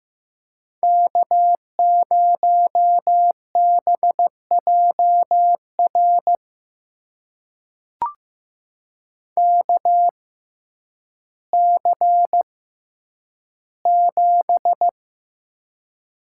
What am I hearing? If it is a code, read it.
Morse code, heard as a keyed tone, about 15 wpm, spaced as for 9 wpm, 700 Hz. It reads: K0BJR KC7
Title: USA Call Signs - 6x Character Spacing - 15wpm